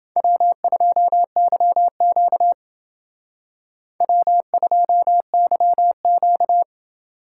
Morse code, heard as a keyed tone, about 29 wpm, 700 Hz. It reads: W2YQ W2YQ